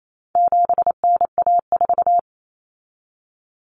Morse code, heard as a keyed tone, about 28 wpm, 700 Hz. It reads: 7NA4
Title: Common Cut Numbers - 4-Digit 28wpm